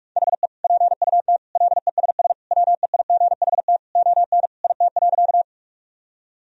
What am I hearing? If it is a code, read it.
Morse code, heard as a keyed tone, about 46 wpm, 700 Hz. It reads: HE PUT LESS WEIGHT ON IT.
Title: Sentences from Top 1000 Words - Review Part 3 - 45wpm